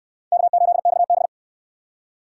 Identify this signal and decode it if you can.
Morse code, heard as a keyed tone, about 64 wpm, 700 Hz. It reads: K8XB